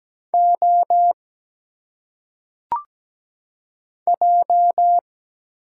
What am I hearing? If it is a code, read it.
Morse code, heard as a keyed tone, about 17 wpm, 700 Hz. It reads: O J
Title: Single Letters - Phonetic Pronunciation 17wpm